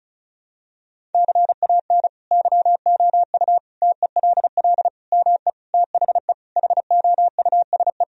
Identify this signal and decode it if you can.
Morse code, heard as a keyed tone, about 34 wpm, 700 Hz. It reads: CAN YOU TELL ME THE HOUSE
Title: Sentences from Top 1000 Words - Review Part 1 - 35wpm